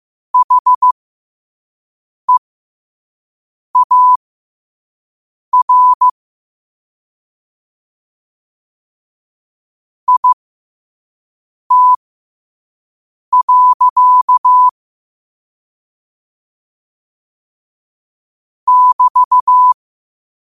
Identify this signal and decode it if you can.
Morse code, heard as a keyed tone, about 15 wpm, spaced as for 5 wpm, 1000 Hz. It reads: HEAR IT. <BT>